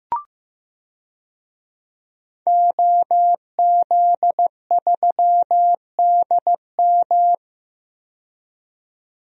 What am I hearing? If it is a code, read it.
Morse code, heard as a keyed tone, about 15 wpm, 700 Hz. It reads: OZ3DM
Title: DXCC-100 Call Signs - 5 Characters - 15wpm